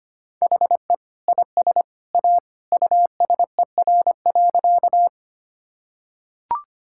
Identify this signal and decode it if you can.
Morse code, heard as a keyed tone, about 25 wpm, 700 Hz. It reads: HE IS A USER.
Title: Sentences from Top 1000 Words - Review Part 5 - 25wpm